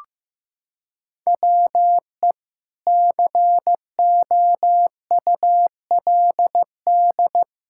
Morse code, heard as a keyed tone, about 16 wpm, 700 Hz. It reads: WE COULD
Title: Pangram Challenge v2 - 15wpm